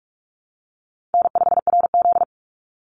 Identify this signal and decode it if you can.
Morse code, heard as a keyed tone, about 44 wpm, 700 Hz. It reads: N5RZ
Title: Call Signs Repeated 2x - 45wpm